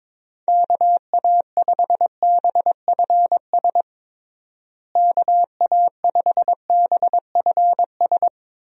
Morse code, heard as a keyed tone, about 22 wpm, 700 Hz. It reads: KA5BFS KA5BFS
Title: USA Call Signs - 6 Characters - Repeated 2x - 22wpm